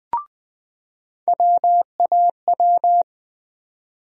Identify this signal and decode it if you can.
Morse code, heard as a keyed tone, about 20 wpm, 700 Hz. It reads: WAW
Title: USA Call Sign Postfixes - 3 Letters 20wpm